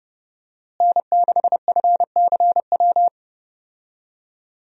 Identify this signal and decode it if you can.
Morse code, heard as a keyed tone, about 32 wpm, 700 Hz. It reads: N6FCW
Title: USA Call Signs - 5 Characters - Repeated 3x - 30wpm